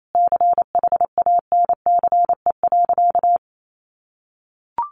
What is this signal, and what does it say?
Morse code, heard as a keyed tone, about 28 wpm, 700 Hz. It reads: CHANCE.
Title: Sentences from Top 800 Words 28wpm